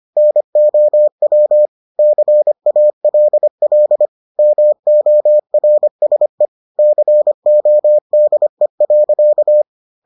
Morse code, heard as a keyed tone, about 25 wpm, 600 Hz. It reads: NOW CALL MORSE CODE.